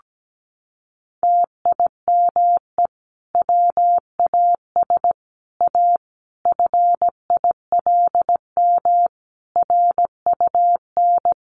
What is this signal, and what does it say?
Morse code, heard as a keyed tone, about 17 wpm, 700 Hz. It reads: TIME WAS A FILM RUN
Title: A Sound Of Thunder 17wpm